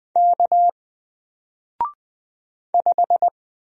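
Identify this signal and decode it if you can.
Morse code, heard as a keyed tone, about 20 wpm, 700 Hz. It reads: K 5
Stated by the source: Single letter and number Morse drill at 20 wpm, from N4LCD.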